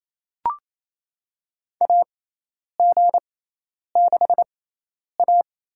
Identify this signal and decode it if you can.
Morse code, heard as a keyed tone, about 27 wpm, spaced as for 10 wpm, 700 Hz. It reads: AG6A
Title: USA Call Signs - 4 Characters - 6x Character Spacing - 28wpm